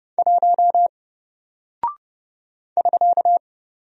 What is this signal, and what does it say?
Morse code, heard as a keyed tone, about 30 wpm, 700 Hz. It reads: 1 <SK>
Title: Single Character (Letter:Number:Punctuation:Prosign) Alt 30wpm